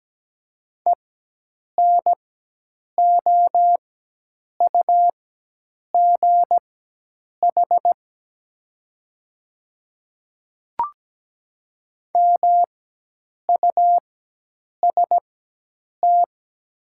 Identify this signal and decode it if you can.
Morse code, heard as a keyed tone, about 17 wpm, spaced as for 8 wpm, 700 Hz. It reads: ENOUGH MUST
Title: 4x Character Spacing - Top 500 Words - 17wpm